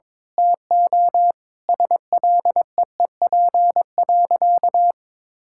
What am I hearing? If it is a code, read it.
Morse code, heard as a keyed tone, about 22 wpm, 700 Hz. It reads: TO SLEEP.
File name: Sentences with 7-Word Phrases - 22wpm